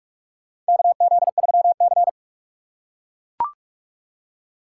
Morse code, heard as a keyed tone, about 41 wpm, 700 Hz. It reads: KZ3C
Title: USA Call Signs - 4 Characters - 2x Character Spacing - 45wpm